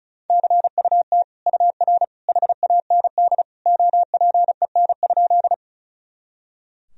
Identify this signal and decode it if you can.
Morse code, heard as a keyed tone, about 36 wpm, 700 Hz. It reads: CUT UR HAND OPEN?
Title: Sentences from Top 300 Words and Abbreviations 35wpm